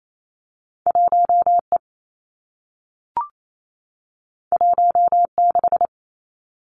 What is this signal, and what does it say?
Morse code, heard as a keyed tone, about 28 wpm, 700 Hz. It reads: 1 E 1 6
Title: Two Challenging Letter-Number Combinations 28wpm